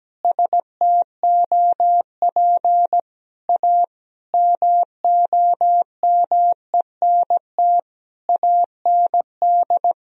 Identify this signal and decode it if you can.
Morse code, heard as a keyed tone, about 17 wpm, 700 Hz. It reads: STOP A MOMENT AND
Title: Sentences from Top 1000 Words - Review Part 1 - 17wpm